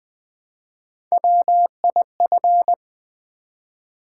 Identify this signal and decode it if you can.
Morse code, heard as a keyed tone, about 20 wpm, 700 Hz. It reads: WIF
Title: Build A Word - 1 Letter At A Time - 4-Letter Words - 20wpm